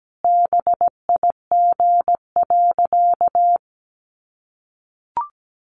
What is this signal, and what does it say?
Morse code, heard as a keyed tone, about 17 wpm, 700 Hz. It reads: BIG.
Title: Sentences from Top 600 Words 17wpm